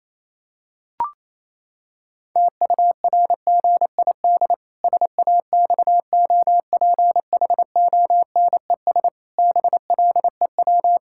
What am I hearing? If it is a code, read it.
Morse code, heard as a keyed tone, about 28 wpm, 700 Hz. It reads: TURGID SAXOPHONES BLEW
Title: Pangram Challenge v2 - 28wpm